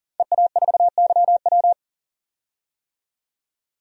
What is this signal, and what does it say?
Morse code, heard as a keyed tone, about 40 wpm, 700 Hz. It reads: EA4YW
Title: DXCC-100 Call Signs - 5 Characters - 40wpm